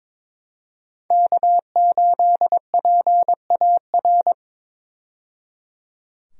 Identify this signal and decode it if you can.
Morse code, heard as a keyed tone, about 22 wpm, 700 Hz. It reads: K8PAR
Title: USA Call Signs - 5 Characters - 22wpm